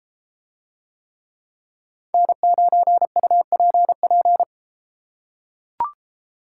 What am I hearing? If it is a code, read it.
Morse code, heard as a keyed tone, about 30 wpm, 700 Hz. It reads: N9UPP